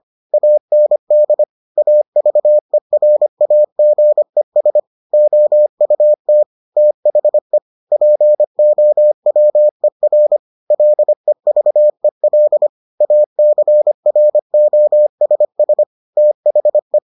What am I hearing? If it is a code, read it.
Morse code, heard as a keyed tone, about 25 wpm, 600 Hz. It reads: AND AVERAGES OUT THE POWER LEVEL ACROSS THE